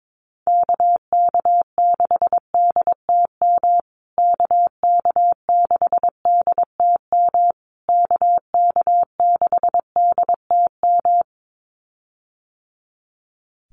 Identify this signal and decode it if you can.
Morse code, heard as a keyed tone, about 22 wpm, 700 Hz. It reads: KK6DTM KK6DTM KK6DTM
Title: USA Call Signs Repeated 3x 22wpm